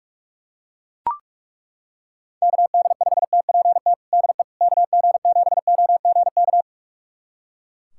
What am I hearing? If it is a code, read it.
Morse code, heard as a keyed tone, about 41 wpm, 700 Hz. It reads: KD5TPT DE KG7OGK
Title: USA Callsign DE Callsign 45wpm